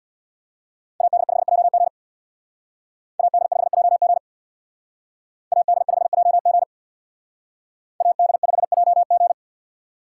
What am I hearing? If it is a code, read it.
Morse code, heard as a keyed tone, about 63 wpm, 700 Hz. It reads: AD5JG AD5JG AD5JG AD5JG